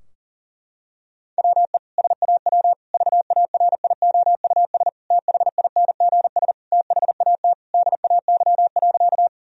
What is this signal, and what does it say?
Morse code, heard as a keyed tone, about 40 wpm, 700 Hz. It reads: WE SAW VARIOUS THINGS THAT DAY.
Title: Sentences from Top 1000 Words - Review Part 4 - 40wpm